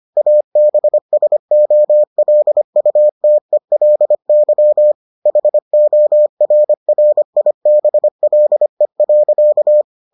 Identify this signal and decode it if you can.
Morse code, heard as a keyed tone, about 25 wpm, 600 Hz. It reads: ABSOLUTELY HORRIBLE.